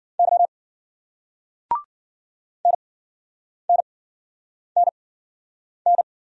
Morse code, heard as a keyed tone, about 53 wpm, spaced as for 9 wpm, 700 Hz. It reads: Y NNNN